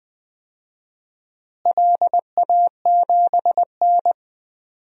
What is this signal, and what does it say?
Morse code, heard as a keyed tone, about 20 wpm, 700 Hz. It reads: LA7N
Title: DXCC-100 Call Signs - 4 Characters - Repeated 3x - 20wpm